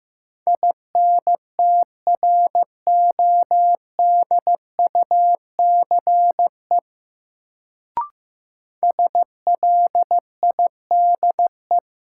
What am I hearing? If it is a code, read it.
Morse code, heard as a keyed tone, about 15 wpm, 700 Hz. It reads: INTRODUCE SLIDE
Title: Sentences from Top 1600 Words - New Word Drill Alt - 15wpm